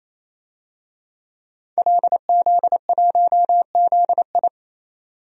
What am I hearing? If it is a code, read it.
Morse code, heard as a keyed tone, about 27 wpm, 700 Hz. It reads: LZ1ZI
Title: DXCC-100 Call Signs - 5 Characters - 28wpm